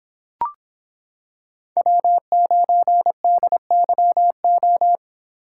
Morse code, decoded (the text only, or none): W9DYO